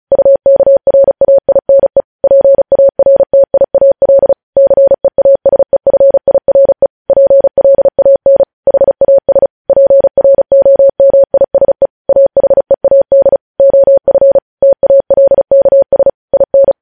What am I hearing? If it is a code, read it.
Morse code, heard as a keyed tone, about 36 wpm, 550 Hz. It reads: UKRAINE PARTIAL CEASEFIRE PLAN HAS PROMISE AHEAD OF TALKS IN
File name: podcast / News headlines-1x-35x35